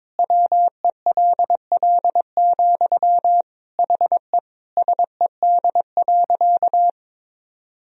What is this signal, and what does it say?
Morse code, heard as a keyed tone, about 23 wpm, 700 Hz. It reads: WELL, HE SED.
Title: Sentences from Top 500 Words and Abbreviations 22wpm